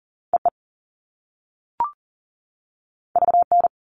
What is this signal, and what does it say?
Morse code, heard as a keyed tone, about 40 wpm, 700 Hz. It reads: EE VN
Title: All Cut Numbers - 2-Digit 40wpm